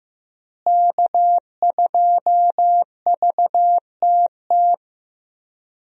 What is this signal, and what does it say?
Morse code, heard as a keyed tone, about 15 wpm, 700 Hz. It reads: K2VTT
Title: USA Call Signs - 5 Characters - Repeated 2x - 15wpm